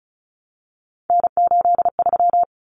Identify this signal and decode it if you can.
Morse code, heard as a keyed tone, about 34 wpm, 700 Hz. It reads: N83
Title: Common Cut Numbers - 3-Digit 35wpm